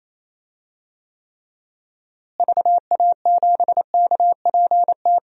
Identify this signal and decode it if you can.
Morse code, heard as a keyed tone, about 28 wpm, 700 Hz. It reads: VA7KPT
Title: DXCC-100 Call Signs - 6 Characters - Repeated 2x - 28wpm